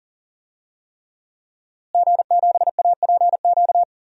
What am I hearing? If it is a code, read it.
Morse code, heard as a keyed tone, about 39 wpm, 700 Hz. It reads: G7APQ
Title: DXCC-100 Call Signs - 5 Characters - Repeated 3x - 40wpm